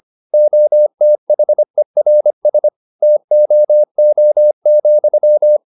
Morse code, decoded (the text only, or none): OTHERS TOO,